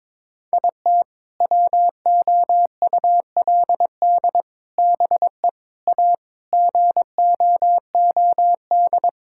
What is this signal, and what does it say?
Morse code, heard as a keyed tone, about 21 wpm, 700 Hz. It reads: IT WOULD BE A GOOD
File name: practice / 22wpm / Sentences with 3-Word Phrases - 22wpm